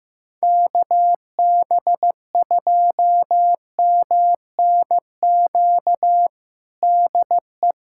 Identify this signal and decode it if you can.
Morse code, heard as a keyed tone, about 15 wpm, 700 Hz. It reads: KB2MNQ DE
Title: USA Callsign DE Callsign - 6 Characters - 15wpm